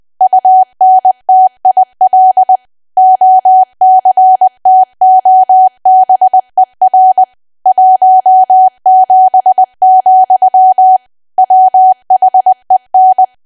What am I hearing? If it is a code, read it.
Morse code, heard as a keyed tone, about 20 wpm, 750 Hz. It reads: UNTIL OCTOBER 17, WHEN
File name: cwa / W1AW-Level3 / 20WPM